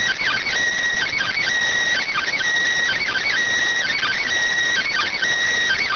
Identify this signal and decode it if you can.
Emergency Alert System (SAME) tones, heard as harsh buzzing bursts. It is heard from 1.0 to 5.2 s.